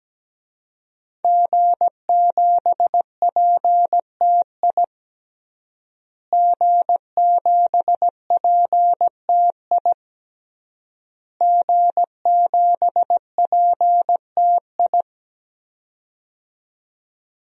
Morse code, heard as a keyed tone, about 17 wpm, 700 Hz. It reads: G7PTI G7PTI G7PTI
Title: DXCC-100 Call Signs - 5 Characters - Repeated 3x - 17wpm